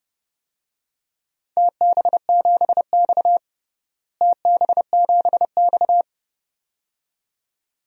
Morse code, heard as a keyed tone, about 32 wpm, 700 Hz. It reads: TB7X TB7X
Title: DXCC-100 Call Signs - 4 Characters - Repeated 2x - 30wpm